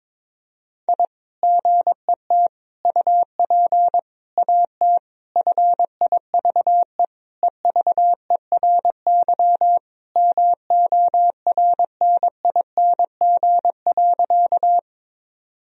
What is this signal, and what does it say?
Morse code, heard as a keyed tone, about 22 wpm, 700 Hz. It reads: I GET UP AT FIVE EVERY MORNING.
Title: Sentences from Top 1000 Words - Review Part 2 - 22wpm